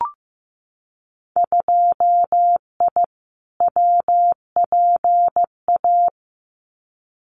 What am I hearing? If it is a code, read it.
Morse code, heard as a keyed tone, about 15 wpm, 700 Hz. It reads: 2I WPA
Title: Winter Field Day Exchanges 15wpm